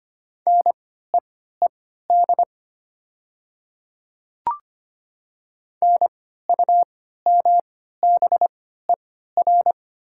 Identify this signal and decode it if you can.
Morse code, heard as a keyed tone, about 25 wpm, spaced as for 14 wpm, 700 Hz. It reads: NEED NUMBER
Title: 3x Character Spacing - Top 200 Words - 25wpm